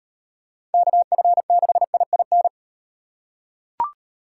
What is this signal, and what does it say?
Morse code, heard as a keyed tone, about 38 wpm, 700 Hz. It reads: KF6IIN